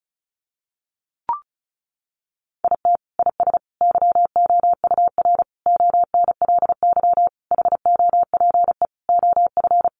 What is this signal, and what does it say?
Morse code, heard as a keyed tone, about 34 wpm, 700 Hz. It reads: IT IS YOUR ONLY HOPE OF
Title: Sentences from Top 1000 Words 35wpm